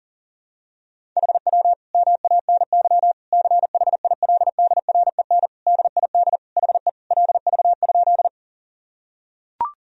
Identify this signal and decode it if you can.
Morse code, heard as a keyed tone, about 40 wpm, 700 Hz. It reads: HW MANY CHILDREN DID HE LV?
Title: Sentences from Top 400 Words and Abbreviations 40wpm